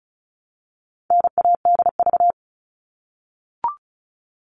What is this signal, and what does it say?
Morse code, heard as a keyed tone, about 34 wpm, 700 Hz. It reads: NADV
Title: All Cut Numbers - 4-Digit 35wpm